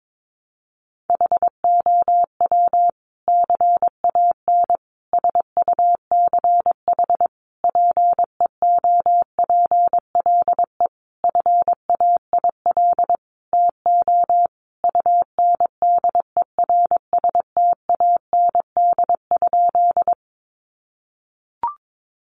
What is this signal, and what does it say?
Morse code, heard as a keyed tone, about 22 wpm, 700 Hz. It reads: HOW CAN SUCH PEOPLE FAIL TO UNDERSTAND?
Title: Sentences from Top 1100 Words 22wpm